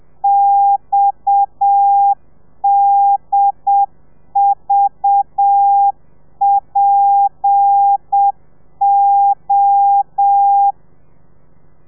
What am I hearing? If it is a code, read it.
Morse code, heard as a keyed tone, about 7 wpm, 800 Hz. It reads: XDVPO
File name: telegrafia / i2tfj / Lez7 30